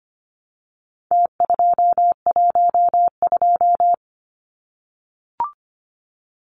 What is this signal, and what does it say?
Morse code, heard as a keyed tone, about 25 wpm, 700 Hz. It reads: T212
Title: Numbers - Contest Style - 4-Digit 25wpm